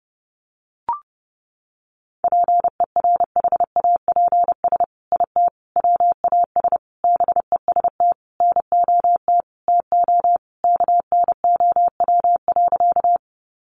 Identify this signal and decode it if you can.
Morse code, heard as a keyed tone, about 30 wpm, 700 Hz. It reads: PERHAPS IT WAS BEST NOT TO KNOW.
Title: Sentences from Top 600 Words 30wpm